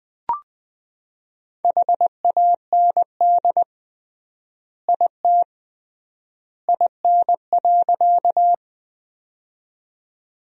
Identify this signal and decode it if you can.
Morse code, heard as a keyed tone, about 20 wpm, 700 Hz. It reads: HAND IT IN.
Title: 3x Word-Spacing - Sentences from Top 200 Words - 20wpm